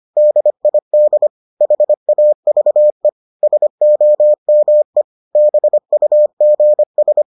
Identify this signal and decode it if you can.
Morse code, heard as a keyed tone, about 25 wpm, 600 Hz. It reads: DID HAVE SOME BUGS